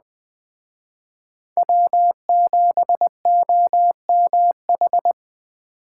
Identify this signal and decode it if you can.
Morse code, heard as a keyed tone, about 20 wpm, 700 Hz. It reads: W7OMH